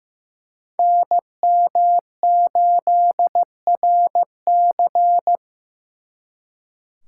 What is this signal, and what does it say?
Morse code, heard as a keyed tone, about 18 wpm, 700 Hz. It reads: NM8RC
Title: Speed Racing - USA 2x2 Call Signs 15wpm